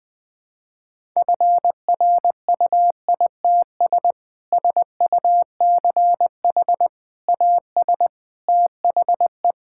Morse code, heard as a keyed tone, about 20 wpm, 700 Hz. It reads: FRUITS SUCH AS THE
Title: Sentences with 3-Word Phrases - Spoken Beginning - 20wpm